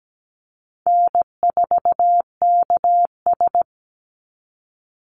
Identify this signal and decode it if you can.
Morse code, heard as a keyed tone, about 17 wpm, 700 Hz. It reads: N4KS
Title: Easy Call Signs - 17wpm